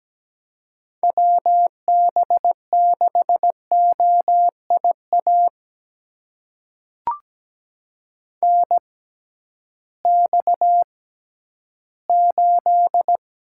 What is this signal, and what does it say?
Morse code, heard as a keyed tone, about 17 wpm, spaced as for 11 wpm, 700 Hz. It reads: WB6OIA NX8
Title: USA Call Signs - 6x Character Spacing - 17wpm